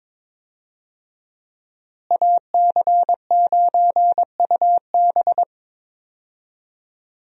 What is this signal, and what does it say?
Morse code, heard as a keyed tone, about 22 wpm, 700 Hz. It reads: AC9UB
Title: USA Call Signs - 5 Characters - Repeated 2x - 22wpm